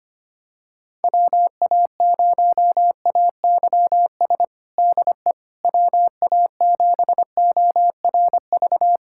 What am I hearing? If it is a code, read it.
Morse code, heard as a keyed tone, about 25 wpm, 700 Hz. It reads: WA0AYS DE WA7ORV
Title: USA Callsign DE Callsign - 6 Characters - 25wpm